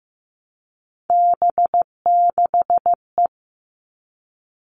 Morse code, heard as a keyed tone, about 15 wpm, 700 Hz. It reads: B6E
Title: All Cut Numbers - 3-Digit 15wpm